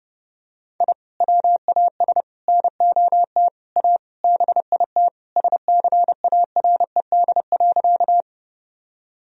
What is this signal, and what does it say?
Morse code, heard as a keyed tone, about 30 wpm, 700 Hz. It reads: I WAS NOT A BIT SCARED.
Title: Sentences from Top 1000 Words - Review Part 4 - 30wpm